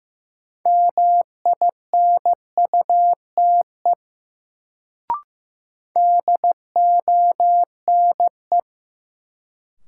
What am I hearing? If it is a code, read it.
Morse code, heard as a keyed tone, about 15 wpm, 700 Hz. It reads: MINUTE DONE
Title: Sentences from Top 500 Words - New Word Drill - 15wpm